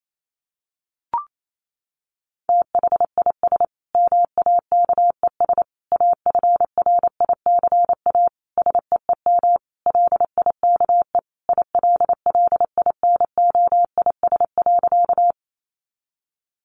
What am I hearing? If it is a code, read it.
Morse code, heard as a keyed tone, about 28 wpm, 700 Hz. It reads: THIS MAKES AFRICA SEEM LIKE ILLINOIS.
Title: A Sound Of Thunder 28wpm